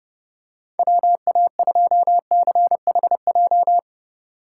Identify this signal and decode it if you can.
Morse code, heard as a keyed tone, about 30 wpm, 700 Hz. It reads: WA2CHJ
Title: USA Call Signs - 6 Characters - 8x Character Spacing - 30wpm